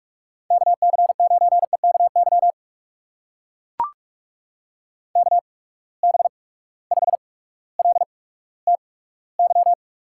Morse code, heard as a keyed tone, about 43 wpm, spaced as for 19 wpm, 700 Hz. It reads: KC9EKY KB5LTY
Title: USA Call Signs - 6 Characters - 8x Character Spacing - 45wpm